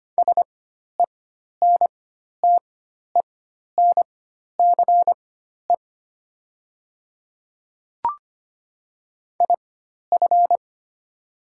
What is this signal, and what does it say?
Morse code, heard as a keyed tone, about 25 wpm, spaced as for 12 wpm, 700 Hz. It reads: SENTENCE IF
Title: 4x Character Spacing - Top 300 Words - 25wpm